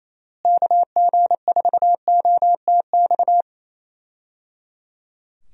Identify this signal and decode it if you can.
Morse code, heard as a keyed tone, about 28 wpm, 700 Hz. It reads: KG4OTX
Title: USA Call Signs - 6 Characters - Repeated 3x - 28wpm